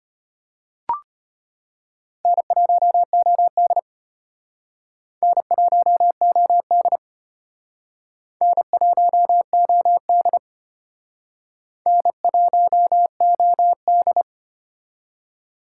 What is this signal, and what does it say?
Morse code, heard as a keyed tone, about 33 wpm, 700 Hz. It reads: N1OD N1OD N1OD N1OD